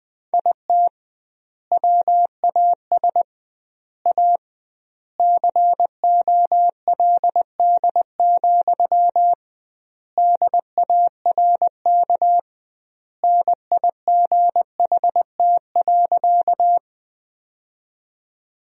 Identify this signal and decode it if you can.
Morse code, heard as a keyed tone, about 21 wpm, 700 Hz. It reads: IT WAS A COLD, DARK NIGHT.
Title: 2x Word-Spacing - Sentences from Top 500 Words - 20wpm